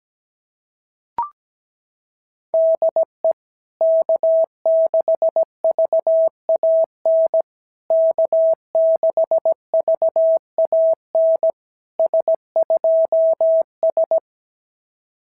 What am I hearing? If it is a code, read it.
Morse code, heard as a keyed tone, about 17 wpm, 650 Hz. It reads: DE K6VAN K6VAN S2S
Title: SOTA QSO Practice 17wpm